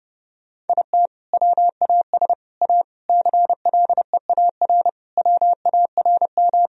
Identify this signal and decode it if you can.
Morse code, heard as a keyed tone, about 30 wpm, 700 Hz. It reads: IT WAS A CLEAR WARM
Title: Sentences from Top 1000 Words - Review Part 2 - 30wpm